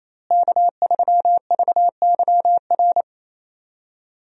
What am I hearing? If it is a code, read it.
Morse code, heard as a keyed tone, about 27 wpm, 700 Hz. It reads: K3VYR
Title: USA Call Signs - 5 Characters - Repeated 3x - 28wpm